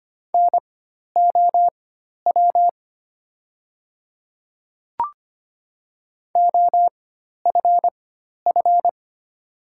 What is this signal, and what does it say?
Morse code, heard as a keyed tone, about 25 wpm, spaced as for 12 wpm, 700 Hz. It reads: NOW OFF
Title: 4x Character Spacing - Top 200 Words - 25wpm